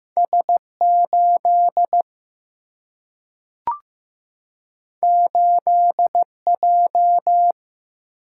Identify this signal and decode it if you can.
Morse code, heard as a keyed tone, about 15 wpm, 700 Hz. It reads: S 8 8 J
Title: Single-Letter Single-Number 15wpm